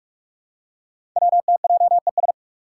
Morse code, heard as a keyed tone, about 44 wpm, 690 Hz. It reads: WTJES
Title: Code Groups - 5 Letters - 45wpm